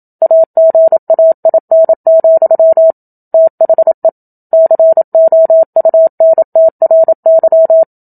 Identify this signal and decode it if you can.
Morse code, heard as a keyed tone, about 27 wpm, 650 Hz. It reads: AGAIN, THE COUNTRY
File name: Book Two / 01-Chapter 1 - 27 WPM